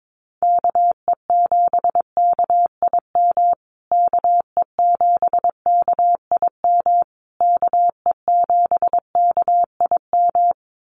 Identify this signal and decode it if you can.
Morse code, heard as a keyed tone, about 22 wpm, 700 Hz. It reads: KE7KIM KE7KIM KE7KIM
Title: USA Call Signs Repeated 3x 22wpm